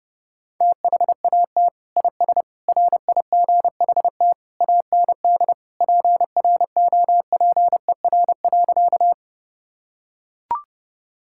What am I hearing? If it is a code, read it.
Morse code, heard as a keyed tone, about 30 wpm, 700 Hz. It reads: THAT IS RIGHT AND PROPER.
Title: Sentences from Top 1000 Words - Review Part 5 - 30wpm